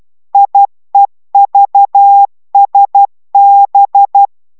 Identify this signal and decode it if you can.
Morse code, heard as a keyed tone, about 12 wpm, 800 Hz. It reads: IEVSB